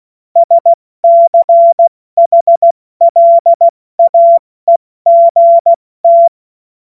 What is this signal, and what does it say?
Morse code, heard as a keyed tone, about 16 wpm, spaced as for 14 wpm, 680 Hz. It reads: SCHLAEGT
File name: Morsekurs / Witze / Witz 09. August 2022 70 F80